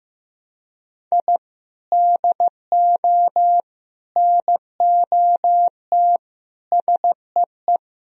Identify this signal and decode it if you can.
Morse code, heard as a keyed tone, about 15 wpm, 700 Hz. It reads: I DO NOT SEE
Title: Speed-Racing - Sets of 5 Words 15wpm